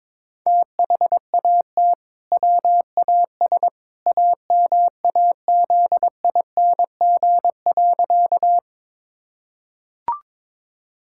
Morse code, heard as a keyed tone, about 21 wpm, 700 Hz. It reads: THAT WAS AMAZING.
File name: practice / 22wpm / Sentences with 2-Word Phrases - 22wpm